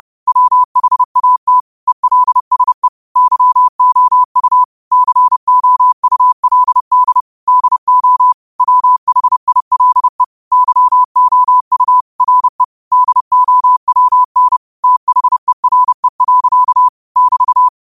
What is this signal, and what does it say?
Morse code, heard as a keyed tone, about 30 wpm, 1000 Hz. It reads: WHAT ELSE YOU COULD DO WHILE YOU RE DOWN THERE. <BT>